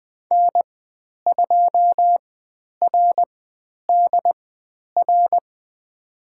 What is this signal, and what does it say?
Morse code, heard as a keyed tone, about 20 wpm, spaced as for 10 wpm, 700 Hz. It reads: N2RDR
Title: 102 Lesson 17g - Letter u - Call Signs 20wpm